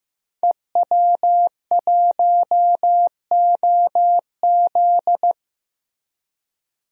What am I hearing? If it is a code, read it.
Morse code, heard as a keyed tone, about 16 wpm, 700 Hz. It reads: EW1OZ
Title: DXCC-100 Call Signs - 5 Characters - Repeated 3x - 15wpm